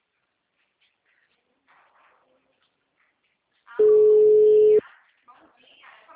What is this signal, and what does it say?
Ringback tone (Europe, 425 Hz), with speech.